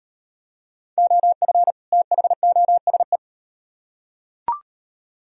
Morse code, heard as a keyed tone, about 38 wpm, 700 Hz. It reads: OF THOSE